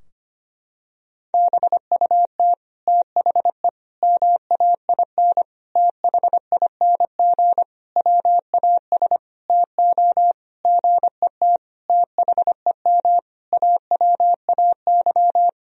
Morse code, heard as a keyed tone, about 25 wpm, 700 Hz. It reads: BUT THE MAIN THING WAS TO GET THEM AWAY
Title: Sentences from Top 1000 Words - Review Part 2 - 25wpm